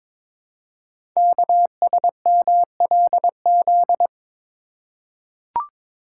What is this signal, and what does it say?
Morse code, heard as a keyed tone, about 22 wpm, 700 Hz. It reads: KSMLZ
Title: Code Groups - 5 Letters - 22wpm